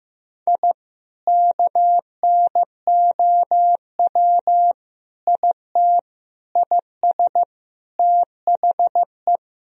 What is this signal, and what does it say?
Morse code, heard as a keyed tone, about 15 wpm, 700 Hz. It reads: I KNOW IT IS THE
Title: Sentences from Top 1000 Words - Review Part 3 - 15wpm